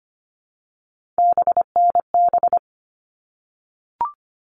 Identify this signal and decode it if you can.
Morse code, heard as a keyed tone, about 25 wpm, 700 Hz. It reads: BNB
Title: All Cut Numbers - 3-Digit 25wpm